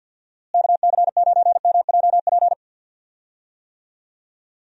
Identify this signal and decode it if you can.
Morse code, heard as a keyed tone, about 46 wpm, 700 Hz. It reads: KC9MJP
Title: USA Call Signs - 6 Characters - Repeated 2x - 50wpm